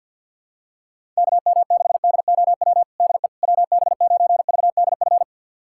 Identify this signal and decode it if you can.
Morse code, heard as a keyed tone, about 52 wpm, 690 Hz. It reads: KM6DOW DE WB9VDF
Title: USA Callsign DE Callsign - 6 Characters - 50wpm